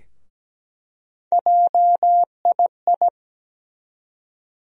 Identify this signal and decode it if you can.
Morse code, heard as a keyed tone, about 17 wpm, 700 Hz. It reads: JII
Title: USA Call Sign Postfixes - 3 Letters 17wpm